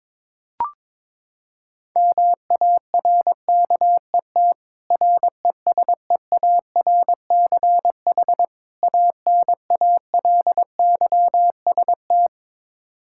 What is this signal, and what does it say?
Morse code, heard as a keyed tone, about 22 wpm, 700 Hz. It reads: MARKET RESEARCH ANALYST
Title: Practice QSO - Exchange 3 - More Info - 22wpm